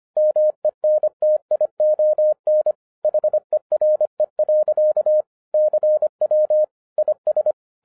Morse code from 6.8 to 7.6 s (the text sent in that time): IS